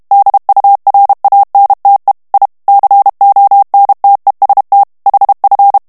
Morse code, heard as a keyed tone, about 32 wpm, 800 Hz. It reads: DURANTE I CONTEST HF
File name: texte clair 2 vitesse 30 mots minute